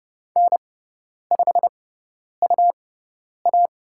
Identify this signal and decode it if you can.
Morse code, heard as a keyed tone, about 30 wpm, spaced as for 10 wpm, 700 Hz. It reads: N5UA
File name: Lesson 18g - Letter m - Call Signs 30wpm